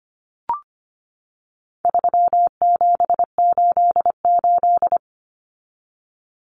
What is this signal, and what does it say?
Morse code, heard as a keyed tone, about 25 wpm, 700 Hz. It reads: 3788
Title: Common Cut Numbers - 4-Digit 25wpm